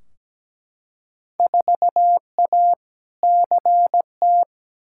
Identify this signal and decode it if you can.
Morse code, heard as a keyed tone, about 17 wpm, 700 Hz. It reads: 4A CT
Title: Field Day Exchanges Encore - 17wpm